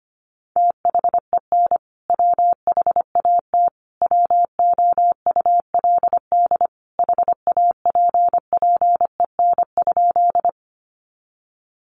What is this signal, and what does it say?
Morse code, heard as a keyed tone, about 25 wpm, 700 Hz. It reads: THEN WHAT WOULD HAPPEN?
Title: Sentences from Top 400 Words 25wpm